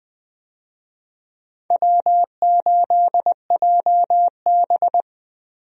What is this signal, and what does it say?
Morse code, heard as a keyed tone, about 20 wpm, 700 Hz. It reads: W8JB
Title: USA Call Signs - 4 Characters - 20wpm